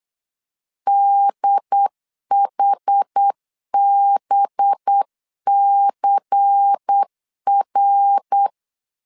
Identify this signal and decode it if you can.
Morse code, heard as a keyed tone, about 8 wpm, 800 Hz. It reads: DHBCR
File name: LEZIONI / 7 GF9Z / 7 G 8wpm 3118